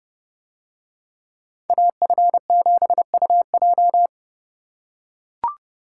Morse code, heard as a keyed tone, about 28 wpm, 700 Hz. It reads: AF7UJ